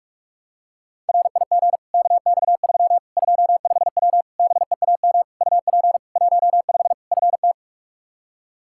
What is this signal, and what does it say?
Morse code, heard as a keyed tone, about 46 wpm, 700 Hz. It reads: RIG KX3 25W BEAM UP 15 FT
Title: K1USN SST QSO Practice 45wpm